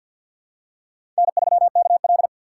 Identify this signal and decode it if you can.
Morse code, heard as a keyed tone, about 51 wpm, 700 Hz. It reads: N3KL
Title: USA Call Signs - 6x Character Spacing - 50wpm